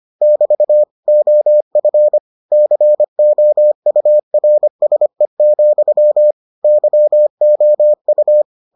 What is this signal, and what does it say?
Morse code, heard as a keyed tone, about 25 wpm, 600 Hz. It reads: <BT> OF COURSE, YOU